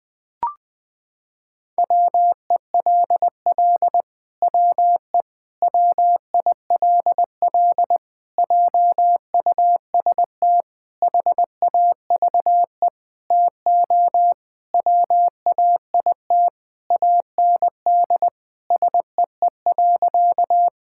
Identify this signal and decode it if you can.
Morse code, heard as a keyed tone, about 21 wpm, 700 Hz. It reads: WELL WE WILL JUST HAVE TO WAIT AND SEE.
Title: Sentences with 7-Word Phrases - 20wpm